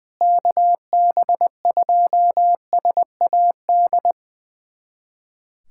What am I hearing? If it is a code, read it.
Morse code, heard as a keyed tone, about 20 wpm, 700 Hz. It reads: KB2SAD
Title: USA Callsign DE Callsign 20wpm